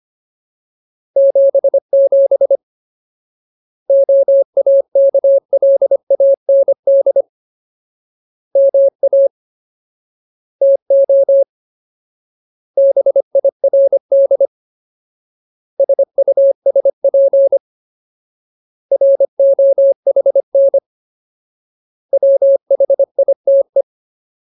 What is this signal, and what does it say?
Morse code, heard as a keyed tone, about 25 wpm, 550 Hz. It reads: 77 OAKLAND MA TO BIRD SUSP ROHN WHITE
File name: Ultimate Word List / 25WPM Ultimate